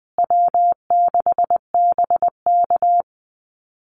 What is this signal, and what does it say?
Morse code, heard as a keyed tone, about 20 wpm, 700 Hz. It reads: W6BK
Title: Call Signs Repeated 3x - 20wpm